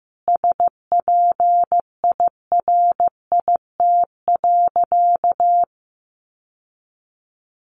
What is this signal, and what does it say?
Morse code, heard as a keyed tone, about 15 wpm, 700 Hz. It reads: SPIRIT.
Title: Sentences from Top 1400 Words 15wpm